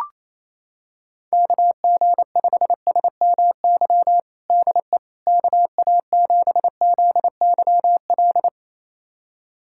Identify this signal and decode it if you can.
Morse code, heard as a keyed tone, about 28 wpm, 700 Hz. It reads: KG5SMY DE KA7ZYL
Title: USA Callsign DE Callsign - 6 Characters - 28wpm